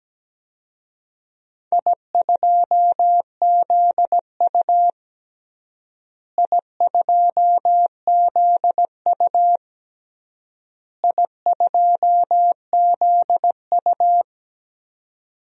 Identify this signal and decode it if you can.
Morse code, heard as a keyed tone, about 17 wpm, 700 Hz. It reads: I2ZU I2ZU I2ZU
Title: DXCC-100 Call Signs - 4 Characters - Repeated 3x - 17wpm